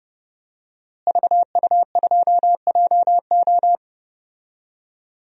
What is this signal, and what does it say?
Morse code, heard as a keyed tone, about 30 wpm, 700 Hz. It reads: VU2JO
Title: DXCC-100 Call Signs - 5 Characters - Repeated 3x - 30wpm